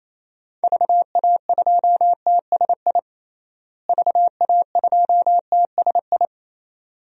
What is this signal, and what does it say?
Morse code, heard as a keyed tone, about 28 wpm, 700 Hz. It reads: VA2TSI VA2TSI
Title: DXCC-100 Call Signs - 6 Characters - Repeated 2x - 28wpm